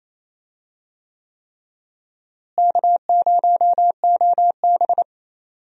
Morse code, heard as a keyed tone, about 27 wpm, 700 Hz. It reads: K0OB
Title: USA Call Signs - 4 Characters - Repeated 3x - 28wpm